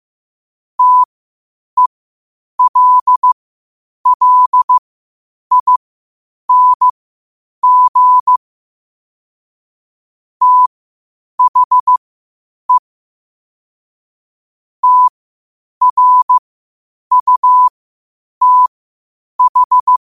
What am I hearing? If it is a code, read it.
Morse code, heard as a keyed tone, about 15 wpm, spaced as for 8 wpm, 1000 Hz. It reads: TELLING THE TRUTH
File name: qotdfiles / 8wpm